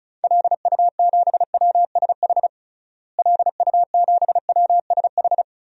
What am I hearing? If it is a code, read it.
Morse code, heard as a keyed tone, about 34 wpm, 700 Hz. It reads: LU7WSH LU7WSH
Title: DXCC-100 Call Signs - 6 Characters - Repeated 3x - 35wpm